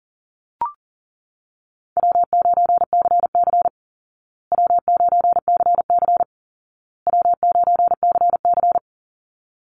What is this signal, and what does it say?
Morse code, heard as a keyed tone, about 40 wpm, 700 Hz. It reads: W9CC W9CC W9CC
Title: Easy Call Signs Repeated 3x - 40wpm